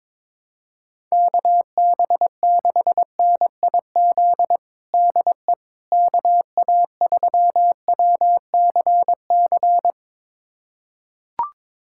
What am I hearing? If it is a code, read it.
Morse code, heard as a keyed tone, about 22 wpm, 700 Hz. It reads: KB6NIZ DE KA3WCC
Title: USA Callsign DE Callsign - 6 Characters - 22wpm